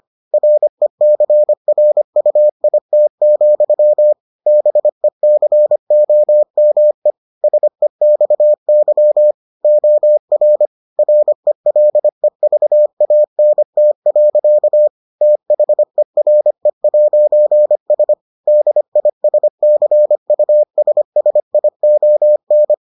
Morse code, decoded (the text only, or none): RECRUIT, BECOME SEXY OR RELEVANT. THERE'S DISCUSSION